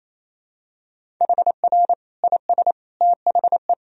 Morse code, heard as a keyed tone, about 27 wpm, 700 Hz. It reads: HR IS THE
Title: Sentences from Top 300 Words and Abbreviations 28wpm